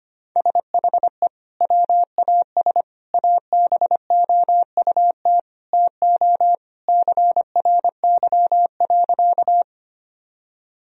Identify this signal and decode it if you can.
Morse code, heard as a keyed tone, about 25 wpm, 700 Hz. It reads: SHE WAS ABOUT TO CRY.
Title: Sentences from Top 1000 Words - Review Part 2 - 25wpm